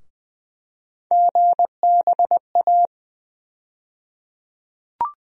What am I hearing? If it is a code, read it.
Morse code, heard as a keyed tone, about 20 wpm, 700 Hz. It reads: GBA
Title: USA Call Sign Postfixes - 3 Letters 20wpm